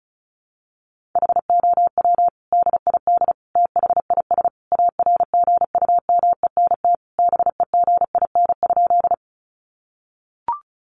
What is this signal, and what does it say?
Morse code, heard as a keyed tone, about 34 wpm, 700 Hz. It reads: HOW DID THIS ARGUMENT BEGIN?
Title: Sentences from Top 1400 Words 35wpm